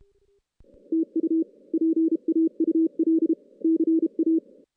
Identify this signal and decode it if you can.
Morse code, heard as a keyed tone, about 33 wpm, 320 Hz. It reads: TU PAUL CA
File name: 2016 JAN CW NAQP / 2016 Jan naqp k5ot - 0-6